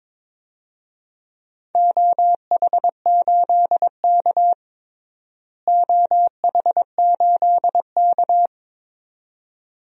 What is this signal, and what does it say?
Morse code, heard as a keyed tone, about 22 wpm, 700 Hz. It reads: OH8K OH8K
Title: DXCC-100 Call Signs - 4 Characters - Repeated 2x - 22wpm